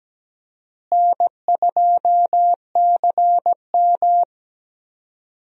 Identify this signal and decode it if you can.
Morse code, heard as a keyed tone, about 17 wpm, 700 Hz. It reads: N2CM
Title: USA Call Signs - 4 Characters - Repeated 3x - 17wpm